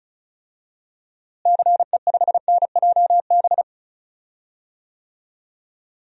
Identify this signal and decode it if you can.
Morse code, heard as a keyed tone, about 34 wpm, 700 Hz. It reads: CE5NJB
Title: DXCC-100 Call Signs - 6 Characters - Repeated 3x - 35wpm